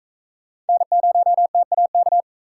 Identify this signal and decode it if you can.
Morse code, heard as a keyed tone, about 42 wpm, 700 Hz. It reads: N0TAK